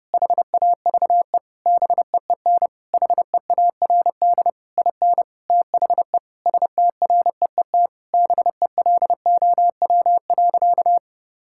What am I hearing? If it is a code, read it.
Morse code, heard as a keyed tone, about 30 wpm, 700 Hz. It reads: HAVE BEEN HEARD IN THE STREET BELOW.
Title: Sentences from Top 1000 Words - Review Part 5 - 30wpm